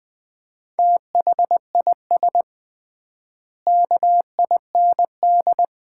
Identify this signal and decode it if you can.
Morse code, heard as a keyed tone, about 20 wpm, 700 Hz. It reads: THIS KIND
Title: 3x Word-Spacing - Sets of 3 Words - 20wpm